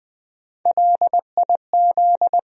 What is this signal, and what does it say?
Morse code, heard as a keyed tone, about 21 wpm, 700 Hz. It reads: LIZ
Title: USA Common Names - Max 3 Letters - 20wpm